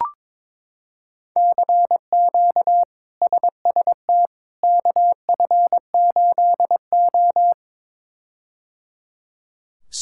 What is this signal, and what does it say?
Morse code, heard as a keyed tone, about 22 wpm, 700 Hz. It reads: CQ SST KF8O
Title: K1USN SST QSO Practice - 22wpm